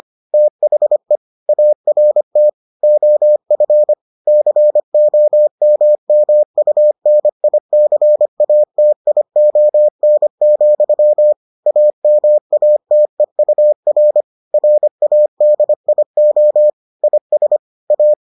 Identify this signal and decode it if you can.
Morse code, heard as a keyed tone, about 25 wpm, 600 Hz. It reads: THE ART OF COMMUNICATION, AMATEUR RADIO IS A